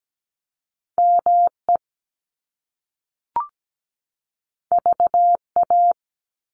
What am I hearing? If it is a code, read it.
Morse code, heard as a keyed tone, about 17 wpm, 700 Hz. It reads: ME VA
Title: ARRL Sections 17wpm